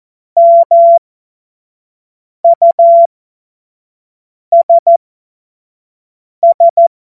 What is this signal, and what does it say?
Morse code, heard as a keyed tone, about 14 wpm, spaced as for 5 wpm, 680 Hz. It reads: MUSS
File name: Morsekurs / Witze / Witz 16 April 2024 25 F70